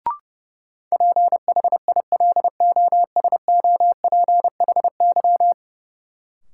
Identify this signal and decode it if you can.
Morse code, heard as a keyed tone, about 30 wpm, 700 Hz. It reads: PHILOSOPHY
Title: Sentences from Top 900 Words - New Word Drill - 30wpm